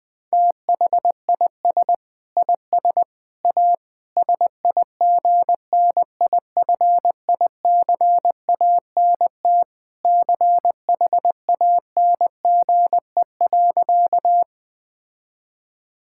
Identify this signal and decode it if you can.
Morse code, heard as a keyed tone, about 20 wpm, 700 Hz. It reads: THIS IS A SIGNIFICANT CHANGE.
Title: Sentences from Top 1000 Words - Review Part 6 - 20wpm